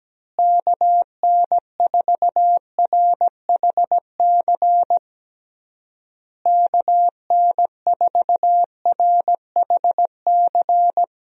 Morse code, heard as a keyed tone, about 17 wpm, 700 Hz. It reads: KN4RHC KN4RHC
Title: USA Call Signs - 6 Characters - Repeated 3x - 17wpm